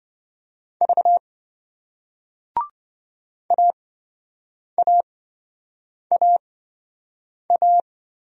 Morse code, heard as a keyed tone, about 26 wpm, spaced as for 7 wpm, 700 Hz. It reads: V AAAA